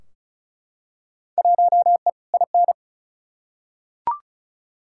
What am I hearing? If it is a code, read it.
Morse code, heard as a keyed tone, about 36 wpm, 700 Hz. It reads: 1E IN
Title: Field Day Exchanges Encore - 35wpm